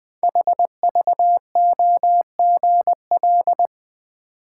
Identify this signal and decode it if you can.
Morse code, heard as a keyed tone, about 20 wpm, 700 Hz. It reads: HVOGL
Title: Code Groups - 5 Letter:Numbers - 20wpm